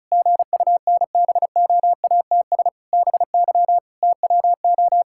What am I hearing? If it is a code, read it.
Morse code, heard as a keyed tone, about 34 wpm, 700 Hz. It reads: GUNBOATS BY TWO
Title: Pangram Challenge v2 - 35wpm